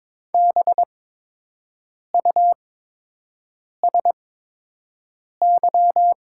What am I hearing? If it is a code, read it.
Morse code, heard as a keyed tone, about 22 wpm, spaced as for 6 wpm, 700 Hz. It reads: BUSY
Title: 8x Character Spacing - Top 500 Words - 22wpm